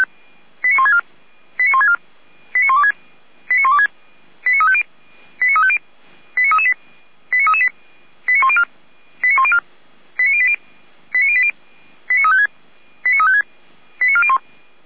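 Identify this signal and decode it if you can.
Five-tone selective call (ZVEI), heard as quick runs of five tones, tones of about 70 ms. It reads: 89164 89164 89126 89126 89350 89350 89338 89338 89104 89104 89080 89080 89356 89356 89401
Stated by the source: ZVEI five-tone selective call 89701 received on an SDR.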